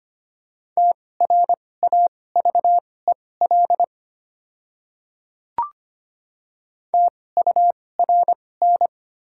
Morse code, heard as a keyed tone, about 25 wpm, spaced as for 18 wpm, 690 Hz. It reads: TRAVEL TURN
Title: 2x Character Spacing - Top 500 Words - 25wpm